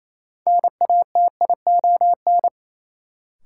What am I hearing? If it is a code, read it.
Morse code, heard as a keyed tone, about 28 wpm, 700 Hz. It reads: NATION
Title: Sets of 3 Words - New Word Drill - 28wpm